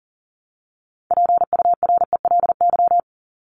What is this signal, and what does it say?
Morse code, heard as a keyed tone, about 40 wpm, 700 Hz. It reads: PURELY
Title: LY Suffix - Top 100 Words with Base Words 40wpm